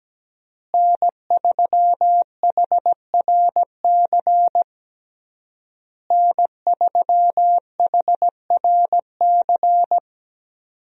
Morse code, heard as a keyed tone, about 17 wpm, 700 Hz. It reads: N3HRC N3HRC
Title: USA Call Signs - 5 Characters - Repeated 3x - 17wpm